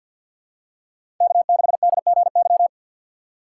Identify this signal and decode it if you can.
Morse code, heard as a keyed tone, about 47 wpm, 700 Hz. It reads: K6DGY
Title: USA Call Signs - 5 Characters - 2x Character Spacing - 50wpm